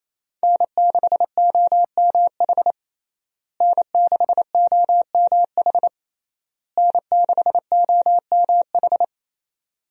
Morse code, heard as a keyed tone, about 27 wpm, 700 Hz. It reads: N6OMH N6OMH N6OMH
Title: USA Call Signs - 5 Characters - Repeated 3x - 28wpm